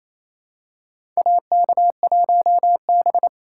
Morse code, heard as a keyed tone, about 27 wpm, 700 Hz. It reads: AK1B
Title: USA Call Signs - 4 Characters - 8x Character Spacing - 28wpm